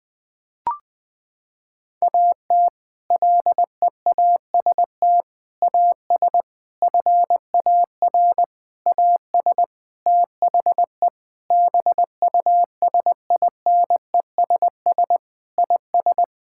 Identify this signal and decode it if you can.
Morse code, heard as a keyed tone, about 20 wpm, 700 Hz. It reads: AT LEAST AS FAR AS THE BUSINESS IS
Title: Sentences from Top 1000 Words - Review Part 6 - 20wpm